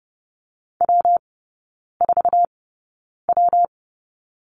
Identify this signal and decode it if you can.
Morse code, heard as a keyed tone, about 30 wpm, spaced as for 9 wpm, 700 Hz. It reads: W4W
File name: Lesson 19e - Letter w - Three Characters 30wpm